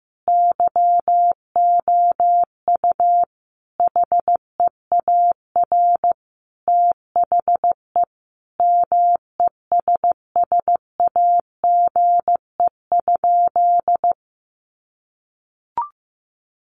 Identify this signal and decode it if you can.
Morse code, heard as a keyed tone, about 15 wpm, 700 Hz. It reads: YOU HEAR THE MESSAGE?
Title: Sentences from Top 1100 Words 15wpm